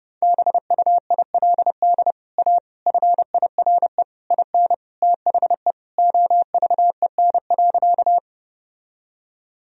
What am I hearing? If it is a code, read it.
Morse code, heard as a keyed tone, about 29 wpm, 700 Hz. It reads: BUILD A FIRE IN THE OVEN.
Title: Sentences from Top 1000 Words - Review Part 4 - 30wpm